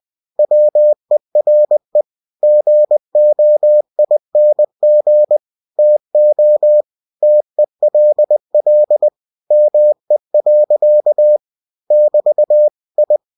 Morse code, heard as a keyed tone, about 20 wpm, 600 Hz. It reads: WERE GOING TO TELL ME. <BT> I
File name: cw-books / winnie-the-pooh / winnie-20wpm-page-0030